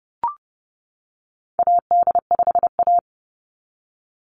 Morse code, heard as a keyed tone, about 30 wpm, 700 Hz. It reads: AD5A
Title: NA SOTA Activator Callsigns 30wpm